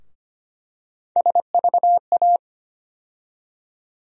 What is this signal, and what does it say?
Morse code, heard as a keyed tone, about 25 wpm, 700 Hz. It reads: SVA
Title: USA Call Sign Postfixes - 3 Letters 25wpm